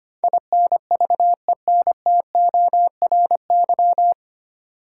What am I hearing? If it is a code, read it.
Morse code, heard as a keyed tone, about 25 wpm, 700 Hz. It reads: INVENTORY
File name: practice / 25wpm / 9 Letter Words - First 3 Letters Given - 25wpm